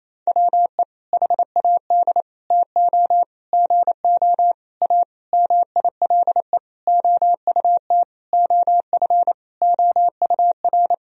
Morse code, heard as a keyed tone, about 28 wpm, 700 Hz. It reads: WE HAD TO GO A MILE OUT OF OUR
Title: Sentences from Top 1000 Words - Review Part 1 - 28wpm